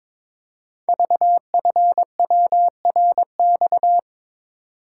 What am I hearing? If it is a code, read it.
Morse code, heard as a keyed tone, about 22 wpm, 690 Hz. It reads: VFWRX
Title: Code Groups - 5 Letters - 22wpm